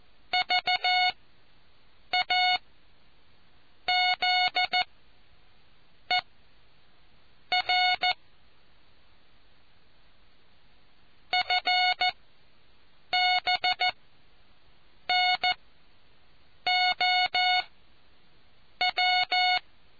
Morse code, heard as a keyed tone, about 14 wpm, spaced as for 6 wpm, 730 Hz. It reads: VAZER FBNOW